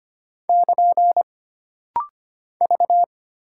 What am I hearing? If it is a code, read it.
Morse code, heard as a keyed tone, about 25 wpm, 700 Hz. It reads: <KN> V
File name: Single Character (Letter:Number:Punctuation:Prosign) - Rapid-Fire - 25wpm